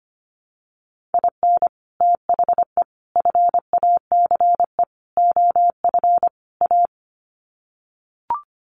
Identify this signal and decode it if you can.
Morse code, heard as a keyed tone, about 25 wpm, 700 Hz. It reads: IN THE FACE OF A
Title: Sets of 5 Words Encore 25wpm